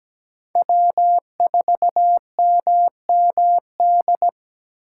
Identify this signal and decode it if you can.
Morse code, heard as a keyed tone, about 17 wpm, 700 Hz. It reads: W4MMD
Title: K1USN SST QSO Practice - 17wpm